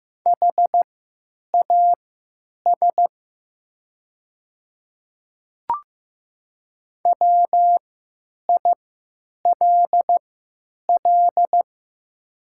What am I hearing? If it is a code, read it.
Morse code, heard as a keyed tone, about 15 wpm, spaced as for 8 wpm, 700 Hz. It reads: HAS WILL
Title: 3x Character Spacing - Top 100 Words - 15wpm